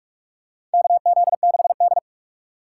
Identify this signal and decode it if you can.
Morse code, heard as a keyed tone, about 46 wpm, 700 Hz. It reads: KZ6D
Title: USA Call Signs - 4 Characters - 3x Character Spacing - 45wpm